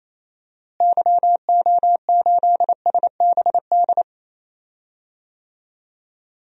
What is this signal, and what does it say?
Morse code, heard as a keyed tone, about 28 wpm, 700 Hz. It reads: YO8SBD
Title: DXCC-100 Call Signs - 6 Characters - Repeated 3x - 28wpm